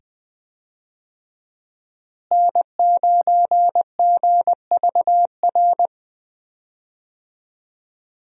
Morse code, heard as a keyed tone, about 20 wpm, 700 Hz. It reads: N9GVR
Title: USA Call Signs - 5 Characters - Repeated 2x - 20wpm